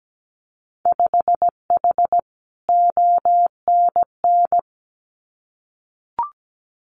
Morse code, heard as a keyed tone, about 17 wpm, 700 Hz. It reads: 5H ONN
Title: Winter Field Day Exchanges Encore 17wpm